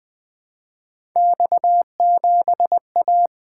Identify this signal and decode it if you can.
Morse code, heard as a keyed tone, about 20 wpm, 700 Hz. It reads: X7A
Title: Three Letter-Number Combinations - 20wpm